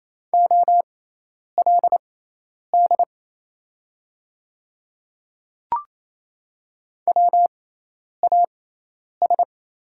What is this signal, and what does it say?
Morse code, heard as a keyed tone, about 28 wpm, spaced as for 10 wpm, 700 Hz. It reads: OLD WAS
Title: 6x Character Spacing - Top 300 Words - 28wpm